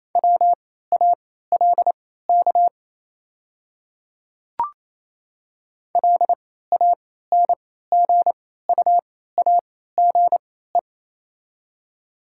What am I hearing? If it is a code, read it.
Morse code, heard as a keyed tone, about 28 wpm, spaced as for 16 wpm, 700 Hz. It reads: WALK LANGUAGE
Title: 3x Character Spacing - Top 500 Words - 28wpm